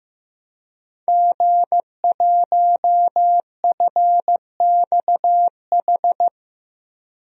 Morse code, heard as a keyed tone, about 15 wpm, 700 Hz. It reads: G1FXH
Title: DXCC-100 Call Signs - 5 Characters - 15wpm